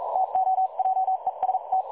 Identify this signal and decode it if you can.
Morse code, heard as a keyed tone, about 42 wpm, spaced as for 36 wpm, 760 Hz. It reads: TO JEST
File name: dlazaawansowanych / nagraniazpasma / websdr recording 3565.0kHz